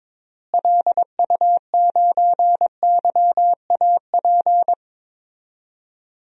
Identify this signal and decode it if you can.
Morse code, heard as a keyed tone, about 22 wpm, 700 Hz. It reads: LU9YAP
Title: DXCC-100 Call Signs - 6 Characters - Repeated 3x - 22wpm